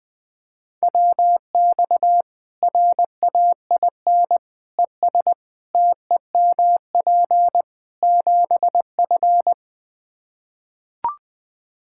Morse code, heard as a keyed tone, about 20 wpm, 700 Hz. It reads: WX RAIN ES TEMP 7F
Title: Practice QSO - Exchange 2 - Info - 20wpm